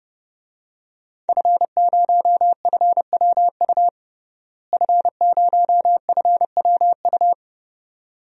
Morse code, heard as a keyed tone, about 32 wpm, 700 Hz. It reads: F0FWU F0FWU
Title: DXCC-100 Call Signs - 5 Characters - Repeated 2x - 30wpm